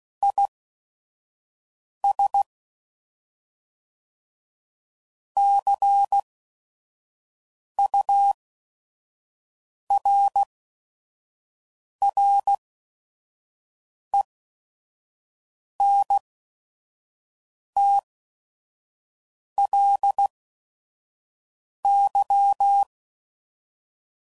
Morse code, heard as a keyed tone, about 16 wpm, spaced as for 5 wpm, 800 Hz. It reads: IS CURRENTLY
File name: CW Code Groups @5WPM / Weather Sets / Weather 5 22